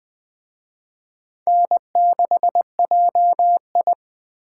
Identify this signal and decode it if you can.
Morse code, heard as a keyed tone, about 21 wpm, 700 Hz. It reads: N6JI